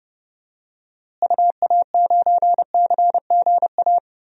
Morse code, heard as a keyed tone, about 31 wpm, 700 Hz. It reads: UA9CGA
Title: DXCC-100 Call Signs - 6 Characters - Repeated 3x - 30wpm